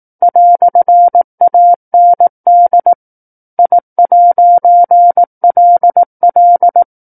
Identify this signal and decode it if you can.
Morse code, heard as a keyed tone, about 18 wpm, 700 Hz. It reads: "AND I'LL
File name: 18 WPM